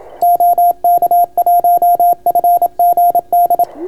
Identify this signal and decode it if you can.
Morse code, heard as a keyed tone, about 28 wpm, 690 Hz. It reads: OK1FGD